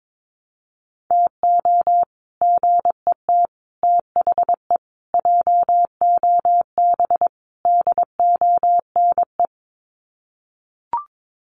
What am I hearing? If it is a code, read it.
Morse code, heard as a keyed tone, about 22 wpm, 700 Hz. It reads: TO GET THE JOB DONE
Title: Sets of 5 Words Encore2 22wpm